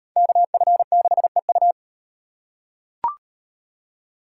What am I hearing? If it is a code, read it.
Morse code, heard as a keyed tone, about 38 wpm, 700 Hz. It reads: KF6EU